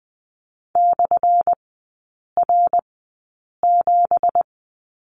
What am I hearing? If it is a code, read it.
Morse code, heard as a keyed tone, about 20 wpm, spaced as for 8 wpm, 700 Hz. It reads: /R7
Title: 178 Lesson 29e - Slash - Three Characters 20wpm